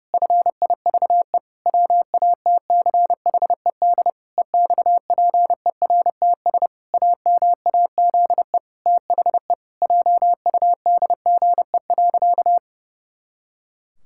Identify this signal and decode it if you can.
Morse code, heard as a keyed tone, about 30 wpm, 700 Hz. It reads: FIVE WATCHED EXPERTS AMAZE THE JUDGE.
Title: Pangram Challenge v2 - 30wpm